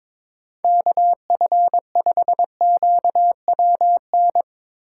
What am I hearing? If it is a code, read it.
Morse code, heard as a keyed tone, about 22 wpm, 700 Hz. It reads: KF5QWN
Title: USA Call Signs - 6 Characters - 8x Character Spacing - 22wpm